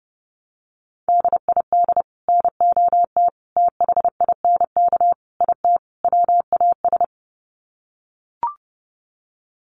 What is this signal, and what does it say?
Morse code, heard as a keyed tone, about 30 wpm, 700 Hz. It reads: DID NOT THINK IT WAS
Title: Sets of 5 Words Encore 30wpm